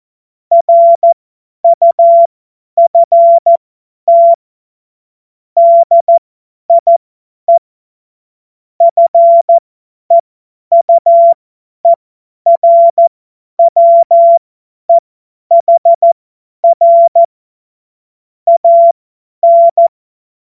Morse code, heard as a keyed tone, about 14 wpm, spaced as for 10 wpm, 680 Hz. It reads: RUFT DIE FEUERWEHR AN